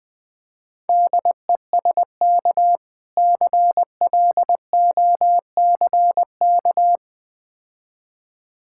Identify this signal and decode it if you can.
Morse code, heard as a keyed tone, about 20 wpm, 700 Hz. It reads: DESK CLOCK
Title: Easy Words - Speed-Racing with Hints - 20wpm